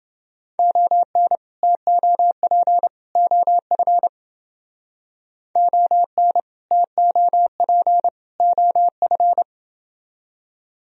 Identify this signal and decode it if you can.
Morse code, heard as a keyed tone, about 29 wpm, 700 Hz. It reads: ON TOP OF ON TOP OF